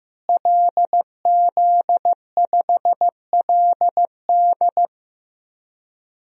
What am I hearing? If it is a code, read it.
Morse code, heard as a keyed tone, about 15 wpm, 700 Hz. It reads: LZ5LD
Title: DXCC-100 Call Signs - 5 Characters - Repeated 2x - 15wpm